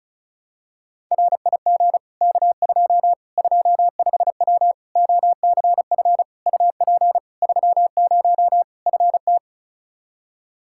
Morse code, heard as a keyed tone, about 36 wpm, 700 Hz. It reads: RIG K2 25W OCF UP 30 FT
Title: Practice QSO - Exchange 2 - Info - 35wpm